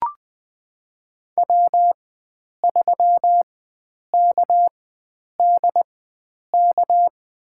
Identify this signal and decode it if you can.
Morse code, heard as a keyed tone, about 20 wpm, spaced as for 9 wpm, 700 Hz. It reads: W3KDK
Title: USA Call Signs - 5 Characters - 4x Character Spacing - 20wpm